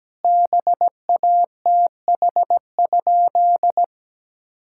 Morse code, heard as a keyed tone, about 17 wpm, 700 Hz. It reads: BATH?
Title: Sentences from Top 1600 Words 17wpm